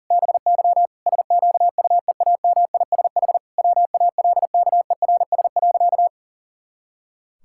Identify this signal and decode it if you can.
Morse code, heard as a keyed tone, about 40 wpm, 700 Hz. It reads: BY SQUEAMISH WALKERS.
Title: Pangram Challenge v2 - 40wpm